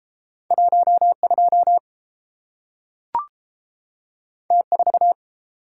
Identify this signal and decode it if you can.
Morse code, heard as a keyed tone, about 30 wpm, 700 Hz. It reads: 1 2 T 4